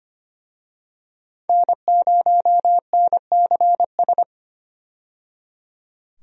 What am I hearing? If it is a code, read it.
Morse code, heard as a keyed tone, about 25 wpm, 690 Hz. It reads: N0NCS
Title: USA Call Signs - 5 Characters - 25wpm